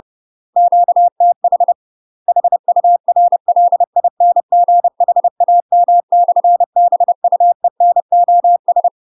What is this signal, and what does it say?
Morse code, heard as a keyed tone, about 30 wpm, 700 Hz. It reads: QTH HURLINGHAM/BUENOS
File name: UK MASTER QSOs TRAINING / 30WPM / UK MASTER QSOs 0022